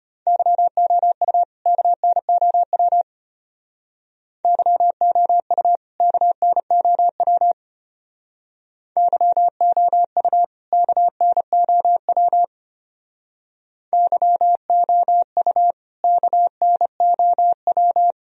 Morse code, heard as a keyed tone, about 31 wpm, 700 Hz. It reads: YOU KNOW YOU KNOW YOU KNOW YOU KNOW
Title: Speed-Racing - Sets of 2 Words 25wpm